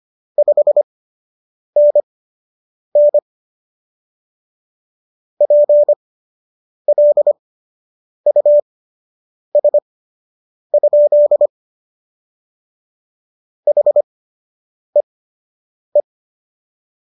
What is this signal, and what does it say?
Morse code, heard as a keyed tone, about 25 wpm, spaced as for 8 wpm, 590 Hz. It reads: 5NN PLUS? HEE
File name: qso006 08